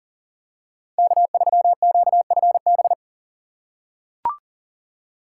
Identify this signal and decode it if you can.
Morse code, heard as a keyed tone, about 37 wpm, 700 Hz. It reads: K3QFB
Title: USA Call Signs - 5 Characters - 4x Character Spacing - 40wpm